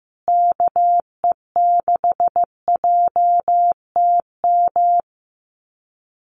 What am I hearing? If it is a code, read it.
Morse code, heard as a keyed tone, about 15 wpm, 700 Hz. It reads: KE6JTM
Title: USA Call Signs Repeated 3x 15wpm